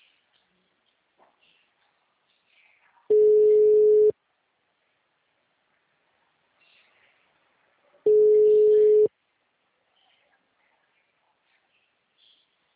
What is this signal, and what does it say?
Ringback tone (Europe, 425 Hz).